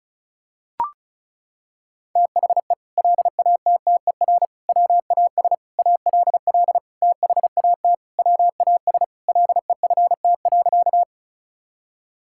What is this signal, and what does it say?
Morse code, heard as a keyed tone, about 34 wpm, 700 Hz. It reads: THE LATTER WAS ALL THAT WAS LEFT.
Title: Sentences from Top 1600 Words 35wpm